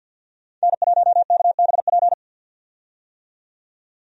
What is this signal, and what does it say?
Morse code, heard as a keyed tone, about 48 wpm, 700 Hz. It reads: N1KBP
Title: USA Call Signs - 5 Characters - 50wpm